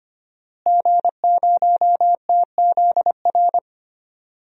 Morse code, heard as a keyed tone, about 25 wpm, 700 Hz. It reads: G0TZR
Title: DXCC-100 Call Signs - 5 Characters - Repeated 2x - 25wpm